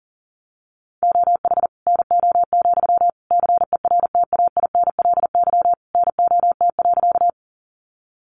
Morse code, heard as a keyed tone, about 37 wpm, 700 Hz. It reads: OH NO, CERTAINLY NOT.
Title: Sentences from Top 900 Words 40wpm